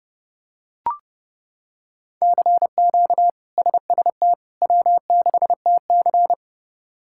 Morse code, heard as a keyed tone, about 30 wpm, 700 Hz. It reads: CQ SST W6TC
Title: K1USN SST QSO Practice - 30wpm